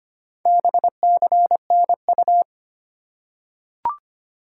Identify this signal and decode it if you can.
Morse code, heard as a keyed tone, about 25 wpm, 700 Hz. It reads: BCNU
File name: Top 90 QSO Elements - 4x Character Spacing - 25wpm